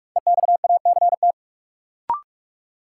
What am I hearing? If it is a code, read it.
Morse code, heard as a keyed tone, about 46 wpm, 700 Hz. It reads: EXACT
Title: Sentences from Top 1300 Words - New Word Drill - 45wpm